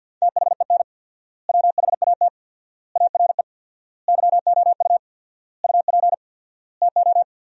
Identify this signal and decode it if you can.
Morse code, heard as a keyed tone, about 50 wpm, 700 Hz. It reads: THEN WHAT ARE YOU UP TO
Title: 4x Word-Spacing - Sentences from Top 100 Words - 50wpm